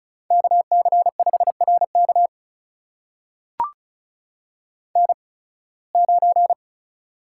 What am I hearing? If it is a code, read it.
Morse code, heard as a keyed tone, about 36 wpm, spaced as for 23 wpm, 700 Hz. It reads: KC5RK N9
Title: USA Call Signs - 5 Characters - 8x Character Spacing - 35wpm